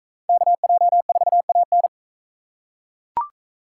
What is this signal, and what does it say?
Morse code, heard as a keyed tone, about 41 wpm, 700 Hz. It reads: KJ4AN